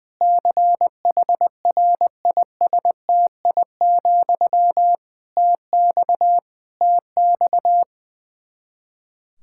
Morse code, heard as a keyed tone, about 20 wpm, 700 Hz. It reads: CHRISTI, TX TX
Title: K1USN SST QSO Practice 20wpm